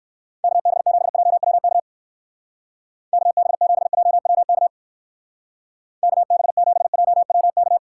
Morse code, heard as a keyed tone, about 58 wpm, 700 Hz. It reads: KB7JWK KB7JWK KB7JWK